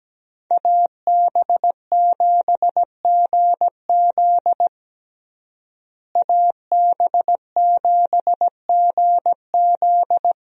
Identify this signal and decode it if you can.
Morse code, heard as a keyed tone, about 17 wpm, 700 Hz. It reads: AB7GZ AB7GZ
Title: USA Call Signs - 5 Characters - Repeated 2x - 17wpm